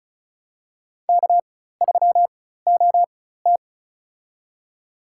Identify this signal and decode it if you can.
Morse code, heard as a keyed tone, about 36 wpm, spaced as for 16 wpm, 700 Hz. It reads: K3OT
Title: USA Call Signs - 4 Characters - 4x Character Spacing - 35wpm